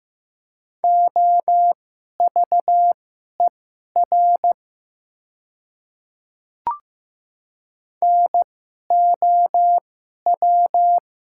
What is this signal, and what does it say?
Morse code, heard as a keyed tone, about 15 wpm, spaced as for 11 wpm, 690 Hz. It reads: OVER NOW
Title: 2x Character Spacing - Top 100 Words - 15wpm